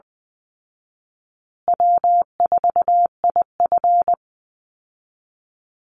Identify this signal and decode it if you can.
Morse code, heard as a keyed tone, about 20 wpm, 700 Hz. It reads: W4IF
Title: Easy Call Signs Repeated 3x - 20wpm